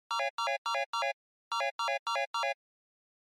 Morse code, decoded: none